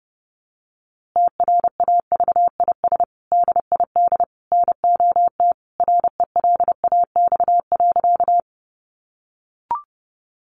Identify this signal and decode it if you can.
Morse code, heard as a keyed tone, about 30 wpm, 700 Hz. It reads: TRAVIS DID NOT RELAX.
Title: A Sound Of Thunder 30wpm